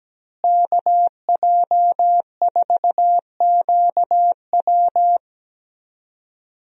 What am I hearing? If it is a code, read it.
Morse code, heard as a keyed tone, about 17 wpm, 700 Hz. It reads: KJ4QW
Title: USA Call Signs - 5 Characters - Repeated 3x - 17wpm